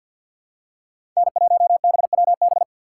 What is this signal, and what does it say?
Morse code, heard as a keyed tone, about 47 wpm, 700 Hz. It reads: N1BWB
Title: USA Call Signs - 5 Characters - 2x Character Spacing - 50wpm